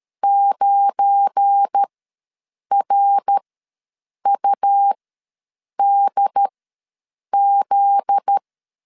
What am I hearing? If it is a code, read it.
Morse code, heard as a keyed tone, about 13 wpm, spaced as for 7 wpm, 800 Hz. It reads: 9RUDZ
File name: LEZIONI / 7 GF9Z / 7 GF9Z ws 12wpm 3321